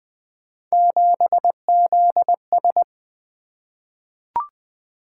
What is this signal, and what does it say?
Morse code, heard as a keyed tone, about 20 wpm, 700 Hz. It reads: 7ZS
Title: Three Letter-Number Combinations - 20wpm